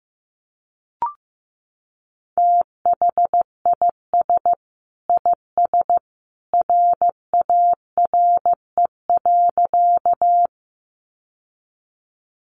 Morse code, heard as a keyed tone, about 15 wpm, 700 Hz. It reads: THIS IS RARE.
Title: Sentences from Top 1100 Words 15wpm